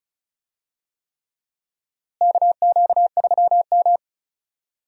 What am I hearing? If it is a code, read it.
Morse code, heard as a keyed tone, about 36 wpm, 700 Hz. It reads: KQ3M
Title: USA Call Signs - 4 Characters - Repeated 3x - 35wpm